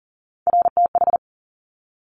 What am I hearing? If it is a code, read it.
Morse code, heard as a keyed tone, about 42 wpm, 690 Hz. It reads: RTH
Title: Top Three Letter Combinations in Words 40wpm